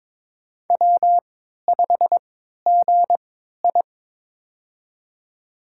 Morse code, heard as a keyed tone, about 22 wpm, spaced as for 13 wpm, 700 Hz. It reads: W5GI
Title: USA Call Signs - 4 Characters - 3x Character Spacing - 22wpm